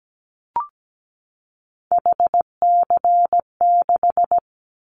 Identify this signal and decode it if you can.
Morse code, heard as a keyed tone, about 17 wpm, 700 Hz. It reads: HC6
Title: Callsign Prefixes for Top 100 DXCC - 17wpm